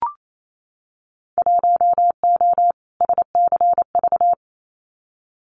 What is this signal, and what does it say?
Morse code, heard as a keyed tone, about 28 wpm, 700 Hz. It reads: 1O SCV
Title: Winter Field Day Exchanges 28wpm